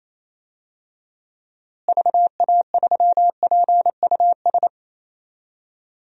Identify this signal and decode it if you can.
Morse code, heard as a keyed tone, about 28 wpm, 700 Hz. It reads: VA3PUS
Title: DXCC-100 Call Signs - 6 Characters - Repeated 2x - 28wpm